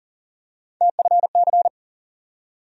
Morse code, heard as a keyed tone, about 40 wpm, 700 Hz. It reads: TFC
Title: USA Call Sign Postfixes - 3 Letters 40wpm